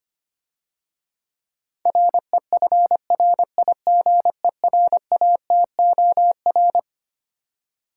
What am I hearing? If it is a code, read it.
Morse code, heard as a keyed tone, about 25 wpm, 700 Hz. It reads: REFRIGERATOR
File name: Easy Words - Speed-Racing with Hints - 25wpm